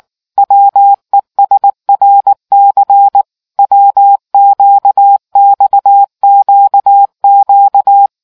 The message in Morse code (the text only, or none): WESRC WQXQQ